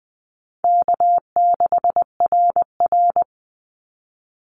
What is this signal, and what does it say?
Morse code, heard as a keyed tone, about 20 wpm, 700 Hz. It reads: K6RR
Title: Call Signs Repeated 2x - 20wpm